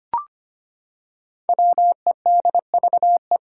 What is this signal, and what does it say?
Morse code, heard as a keyed tone, about 25 wpm, 700 Hz. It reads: WEDVE
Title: Five Challenging Letter-Number Combinations 25wpm